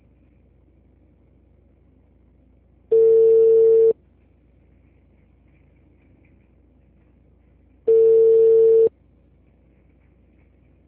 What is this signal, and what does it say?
Ringback tone (China, 450 Hz).